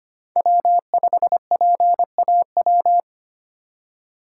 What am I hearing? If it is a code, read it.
Morse code, heard as a keyed tone, about 25 wpm, 700 Hz. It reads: W5PAW
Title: USA Call Signs - 5 Characters - 4x Character Spacing - 25wpm